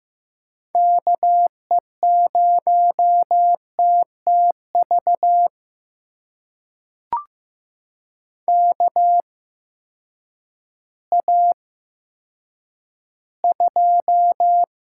Morse code, heard as a keyed tone, about 15 wpm, spaced as for 9 wpm, 700 Hz. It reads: KE0TTV KA2
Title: USA Call Signs - 6 Characters - 8x Character Spacing - 15wpm